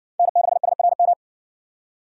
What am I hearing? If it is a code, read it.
Morse code, heard as a keyed tone, about 59 wpm, 700 Hz. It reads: N6SRM